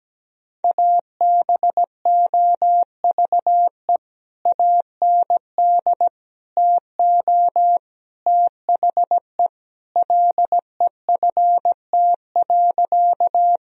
Morse code, heard as a keyed tone, about 17 wpm, 700 Hz. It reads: ABOVE AND TO THE LEFT.
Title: Sentences from Top 1000 Words - Review Part 2 - 17wpm